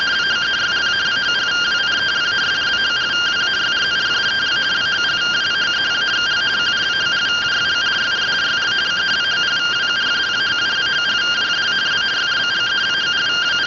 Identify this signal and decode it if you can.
Two-tone teleprinter signal, about 170 Hz shift.
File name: findings / CT 205 ax mips, iDose 3 0007.gen-11 rtty rx